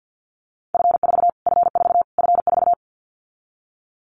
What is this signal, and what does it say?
Morse code, heard as a keyed tone, about 50 wpm, 700 Hz. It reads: F4 F4 F4
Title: Callsign Prefixes for Top 25 DXCC - Repeated 3x - 50wpm